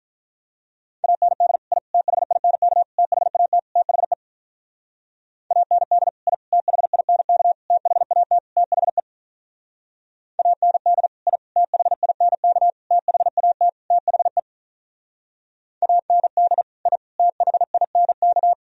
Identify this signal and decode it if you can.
Morse code, heard as a keyed tone, about 46 wpm, 700 Hz. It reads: AND I THINK THAT THE AND I THINK THAT THE AND I THINK THAT THE AND I THINK